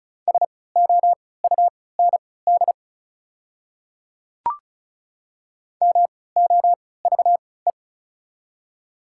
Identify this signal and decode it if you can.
Morse code, heard as a keyed tone, about 34 wpm, spaced as for 20 wpm, 690 Hz. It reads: SOUND MOVE
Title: 3x Character Spacing - Top 500 Words - 35wpm